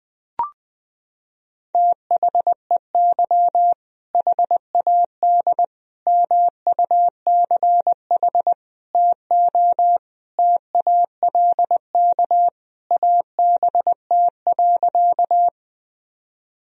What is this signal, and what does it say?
Morse code, heard as a keyed tone, about 20 wpm, 700 Hz. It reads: THEY HAD MUCH TO TALK ABT.
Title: Sentences from Top 400 Words and Abbreviations 20wpm